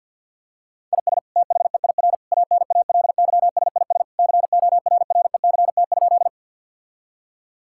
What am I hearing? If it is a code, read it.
Morse code, heard as a keyed tone, about 51 wpm, 700 Hz. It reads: IS THEIR ANALYSIS CORRECT?
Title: Sentences from Top 1000 Words - Review Part 5 - 50wpm